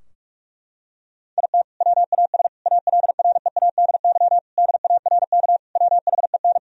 Morse code, heard as a keyed tone, about 46 wpm, 700 Hz. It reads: IT WAS ALREADY DARK WHEN
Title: Sentences from Top 1000 Words - Review Part 2 - 45wpm